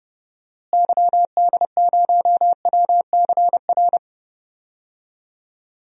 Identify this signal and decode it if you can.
Morse code, heard as a keyed tone, about 30 wpm, 700 Hz. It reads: YD0WCR
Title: DXCC-100 Call Signs - 6 Characters - Repeated 3x - 30wpm